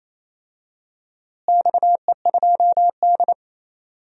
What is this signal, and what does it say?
Morse code, heard as a keyed tone, about 28 wpm, 700 Hz. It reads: XE2D
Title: DXCC-100 Call Signs - 4 Characters - Repeated 3x - 28wpm